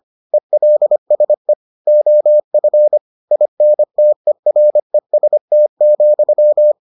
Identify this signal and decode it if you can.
Morse code, heard as a keyed tone, about 25 wpm, 600 Hz. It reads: ELSE OF INTEREST,